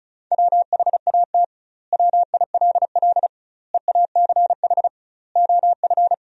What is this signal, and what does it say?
Morse code, heard as a keyed tone, about 34 wpm, 700 Hz. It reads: WHAT WILL EACH OF
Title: 2x Word-Spacing - Sentences from Top 200 Words - 35wpm